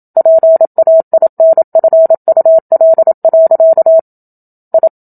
Morse code, heard as a keyed tone, about 27 wpm, 650 Hz. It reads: PAINFUL. I
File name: Book Two / 01-Chapter 1 - 27 WPM